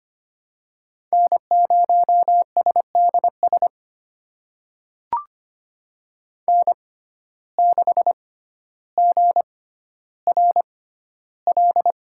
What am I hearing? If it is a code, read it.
Morse code, heard as a keyed tone, about 25 wpm, spaced as for 13 wpm, 700 Hz. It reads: N0SDS N6GRL
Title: USA Call Signs - 6x Character Spacing - 25wpm